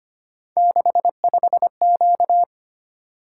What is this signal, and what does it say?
Morse code, heard as a keyed tone, about 25 wpm, 700 Hz. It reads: 65Q
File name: Three Letter-Number Combinations - 25wpm